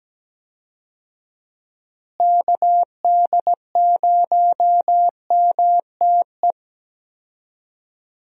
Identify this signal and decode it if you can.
Morse code, heard as a keyed tone, about 17 wpm, 700 Hz. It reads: KD0MTE
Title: USA Call Signs - 6 Characters - Repeated 2x - 17wpm